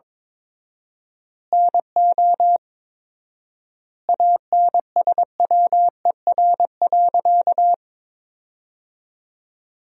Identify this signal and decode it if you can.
Morse code, heard as a keyed tone, about 21 wpm, 700 Hz. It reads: NO ANSWER.
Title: 4x Word-Spacing - Sentences from Top 500 Words - 22wpm